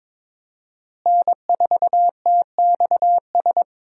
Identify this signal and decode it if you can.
Morse code, heard as a keyed tone, about 22 wpm, 700 Hz. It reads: N4TXS
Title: USA Call Signs - 5 Characters - Repeated 3x - 22wpm